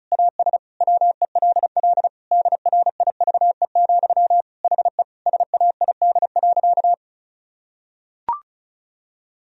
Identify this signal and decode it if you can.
Morse code, heard as a keyed tone, about 34 wpm, 700 Hz. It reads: AS WELL DRIVE, HE SAID.
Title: Sentences from Top 1000 Words - Review Part 2 - 35wpm